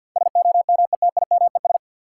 Morse code, heard as a keyed tone, about 53 wpm, 700 Hz. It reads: SOMETIMES
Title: Sentences from Top 600 Words - New Word Drill - 50wpm